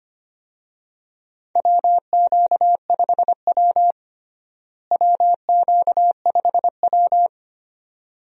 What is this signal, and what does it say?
Morse code, heard as a keyed tone, about 25 wpm, 700 Hz. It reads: WQ5W WQ5W
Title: USA Call Signs - 4 Characters - Repeated 2x - 25wpm